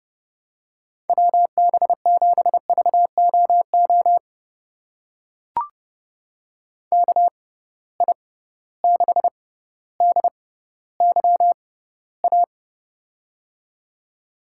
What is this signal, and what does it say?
Morse code, heard as a keyed tone, about 30 wpm, spaced as for 15 wpm, 700 Hz. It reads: WB7VOO KI6DYA
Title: USA Call Signs - 6 Characters - 6x Character Spacing - 30wpm